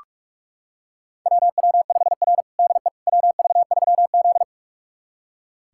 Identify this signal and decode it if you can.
Morse code, heard as a keyed tone, about 46 wpm, 690 Hz. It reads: WW5R DE WV3Z
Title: USA Callsign DE Callsign - 4 Characters - 45wpm